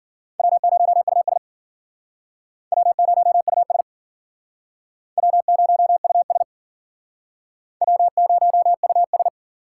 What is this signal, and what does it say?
Morse code, heard as a keyed tone, about 49 wpm, 700 Hz. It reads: W0US W0US W0US W0US
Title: Speed Racing - USA 1x2 Call Signs 40wpm